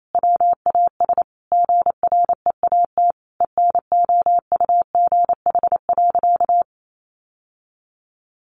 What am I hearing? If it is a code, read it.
Morse code, heard as a keyed tone, about 28 wpm, 700 Hz. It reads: WAS GREAT ENOUGH.
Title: Sentences from Top 1500 Words 28wpm